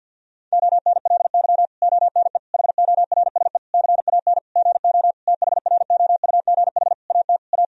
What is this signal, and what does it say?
Morse code, heard as a keyed tone, about 49 wpm, 700 Hz. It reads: ONLY ONE HORSE CAN GO THROUGH AT A
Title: Sentences from Top 1000 Words - Review Part 1 - 50wpm